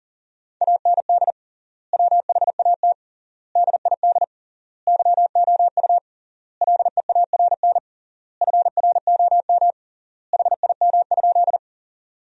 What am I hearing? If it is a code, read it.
Morse code, heard as a keyed tone, about 40 wpm, 700 Hz. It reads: AND WHAT DID YOU LEARN FROM HIM?
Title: 3x Word-Spacing - Sentences from Top 300 Words - 40wpm